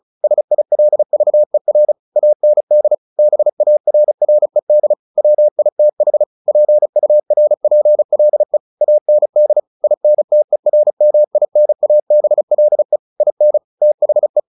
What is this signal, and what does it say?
Morse code, heard as a keyed tone, about 34 wpm, 600 Hz. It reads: SILVER AND BARRED WITH PURPLE AND INTERMINABLE IN THE
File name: the old man and the sea 2 35wpm